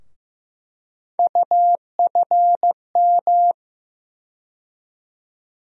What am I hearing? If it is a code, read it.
Morse code, heard as a keyed tone, about 15 wpm, 700 Hz. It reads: UFM
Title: USA Call Sign Postfixes - 3 Letters 15wpm